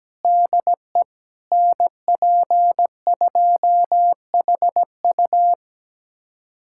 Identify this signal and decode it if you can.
Morse code, heard as a keyed tone, about 17 wpm, 700 Hz. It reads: DE NP2HU
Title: USA Callsign DE Callsign - 5 Characters - 17wpm